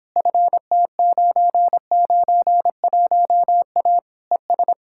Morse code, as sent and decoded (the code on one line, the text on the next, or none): ..-. - ----. ----. .---- .- / . ...
FT991A ES